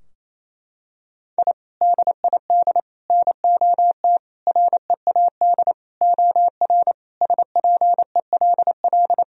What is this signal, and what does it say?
Morse code, heard as a keyed tone, about 27 wpm, 700 Hz. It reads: I DID NOT READ OR SPELL
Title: Sentences from Top 1000 Words - Review Part 3 - 28wpm